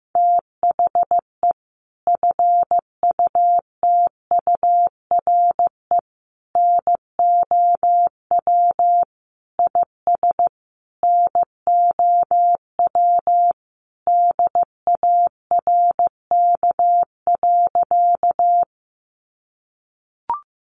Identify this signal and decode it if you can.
Morse code, heard as a keyed tone, about 15 wpm, 700 Hz. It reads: THE FUTURE NOW IS NOW DARK.
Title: Sentences from Top 700 Words 15wpm